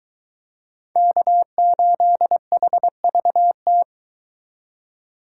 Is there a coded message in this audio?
Yes: Morse code from 1.0 to 3.8 s, heard as a keyed tone, about 23 wpm, 700 Hz. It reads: K8HVT